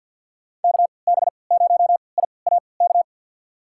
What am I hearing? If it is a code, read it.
Morse code, heard as a keyed tone, about 51 wpm, spaced as for 28 wpm, 700 Hz. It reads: KB0IAK
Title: USA Call Signs - 6 Characters - 3x Character Spacing - 50wpm